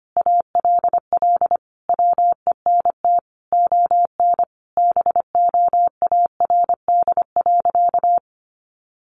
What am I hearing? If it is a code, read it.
Morse code, heard as a keyed tone, about 25 wpm, 700 Hz. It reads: ALL WENT ON BOARD.
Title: Sentences from Top 1400 Words 25wpm